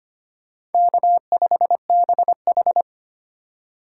Morse code, heard as a keyed tone, about 33 wpm, 700 Hz. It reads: K5BH